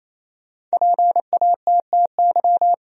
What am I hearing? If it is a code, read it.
Morse code, heard as a keyed tone, about 28 wpm, 700 Hz. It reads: PATTY
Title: USA Common Names - Max 5 Letters - 28wpm